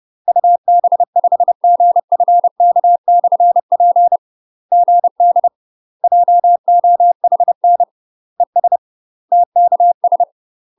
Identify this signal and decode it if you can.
Morse code, heard as a keyed tone, about 30 wpm, 700 Hz. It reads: UB5GFK/P GD JOHN ES TKS
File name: UK MASTER QSOs TRAINING/30WPM